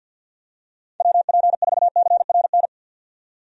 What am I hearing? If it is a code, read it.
Morse code, heard as a keyed tone, about 47 wpm, 700 Hz. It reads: WP4CRN
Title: USA Call Signs - 6 Characters - 50wpm